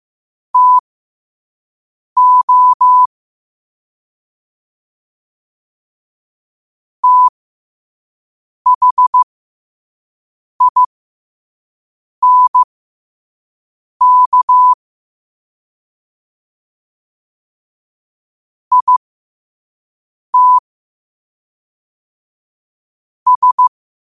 Morse code, heard as a keyed tone, about 15 wpm, spaced as for 5 wpm, 1000 Hz. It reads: TO THINK IT S
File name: qotdfiles / 5wpm